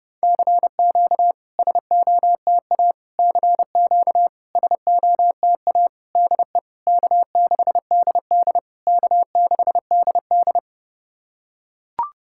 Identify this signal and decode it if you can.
Morse code, heard as a keyed tone, about 30 wpm, 690 Hz. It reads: CQ SOTA CQ SOTA DE K6DD K6DD
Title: SOTA QSO Practice 30wpm